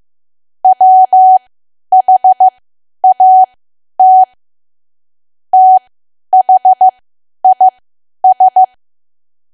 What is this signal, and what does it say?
Morse code, heard as a keyed tone, about 15 wpm, spaced as for 10 wpm, 750 Hz. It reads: WHAT THIS
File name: cwa / W1AW-Level2 / 10WPM